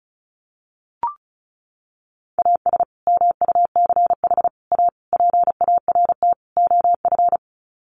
Morse code, heard as a keyed tone, about 34 wpm, 700 Hz. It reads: AS MUCH A PART OF
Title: Sets of 5 Words Encore2 35wpm